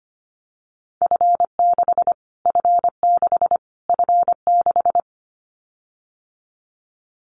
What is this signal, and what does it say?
Morse code, heard as a keyed tone, about 25 wpm, 700 Hz. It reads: F6 F6 F6
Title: Callsign Prefixes for Top 25 DXCC - Repeated 3x - 25wpm